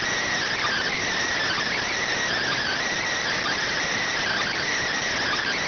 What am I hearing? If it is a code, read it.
Emergency Alert System (SAME) tones, heard as harsh buzzing bursts. It is heard from 0.6 to 4.7 s.